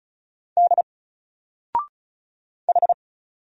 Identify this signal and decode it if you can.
Morse code, heard as a keyed tone, about 35 wpm, 700 Hz. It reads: D H
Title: Single Letter-Number 35wpm